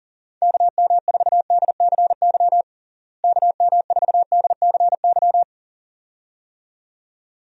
Morse code, heard as a keyed tone, about 40 wpm, 700 Hz. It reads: KM4DCY KM4DCY
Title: USA Call Signs - 6 Characters - Repeated 2x - 40wpm